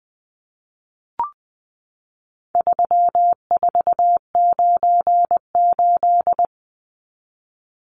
Morse code, heard as a keyed tone, about 20 wpm, 700 Hz. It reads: 3498
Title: Numbers - 4-Digit 20wpm